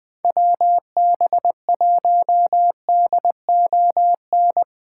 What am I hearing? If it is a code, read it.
Morse code, heard as a keyed tone, about 23 wpm, 700 Hz. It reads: WB1DON
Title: Speed Racing - USA 2x3 Call Signs 20wpm